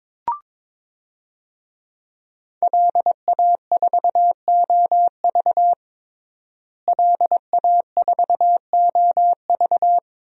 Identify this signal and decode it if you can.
Morse code, heard as a keyed tone, about 22 wpm, 700 Hz. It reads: LA4OV LA4OV
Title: DXCC-100 Call Signs - 5 Characters - Repeated 3x - 22wpm